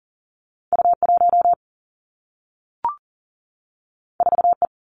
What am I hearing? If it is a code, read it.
Morse code, heard as a keyed tone, about 39 wpm, 700 Hz. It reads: U 1 4 E
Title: Two Challenging Letter-Number Combinations 40wpm